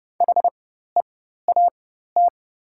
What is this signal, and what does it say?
Morse code, heard as a keyed tone, about 29 wpm, spaced as for 14 wpm, 690 Hz. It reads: HEAT
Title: 4x Character Spacing - Top 500 Words - 30wpm